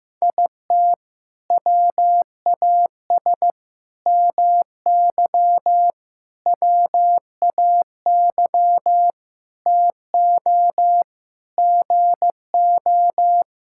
Morse code, heard as a keyed tone, about 15 wpm, 700 Hz. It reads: IT WAS MY WAY TO GO
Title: Sentences from Top 1600 Words 15wpm